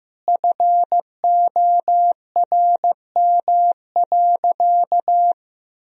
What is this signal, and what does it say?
Morse code, heard as a keyed tone, about 15 wpm, 700 Hz. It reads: FORM.
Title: Sentences from Top 1000 Words - Review Part 1 - 15wpm